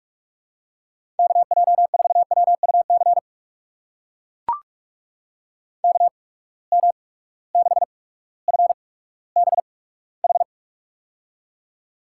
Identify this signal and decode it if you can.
Morse code, heard as a keyed tone, about 44 wpm, spaced as for 19 wpm, 700 Hz. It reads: KJ4WUC KM6FBH
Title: USA Call Signs - 6 Characters - 8x Character Spacing - 45wpm